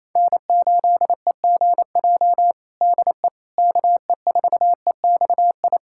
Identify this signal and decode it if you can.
Morse code, heard as a keyed tone, about 28 wpm, 700 Hz. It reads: N8EGJ DE KE4EXI
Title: USA Callsign DE Callsign 28wpm